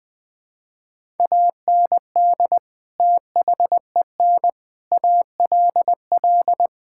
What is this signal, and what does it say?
Morse code, heard as a keyed tone, about 21 wpm, 700 Hz. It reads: AND THEN ALL
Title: Sets of 6 Words - 20wpm